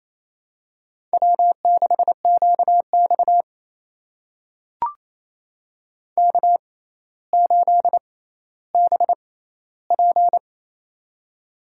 Morse code, heard as a keyed tone, about 28 wpm, spaced as for 14 wpm, 700 Hz. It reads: W6QX K8BP
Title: USA Call Signs - 4 Characters - 6x Character Spacing - 28wpm